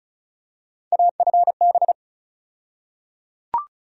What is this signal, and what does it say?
Morse code, heard as a keyed tone, about 34 wpm, 700 Hz. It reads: AFB
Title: USA Call Sign Postfixes - 3 Letters 35wpm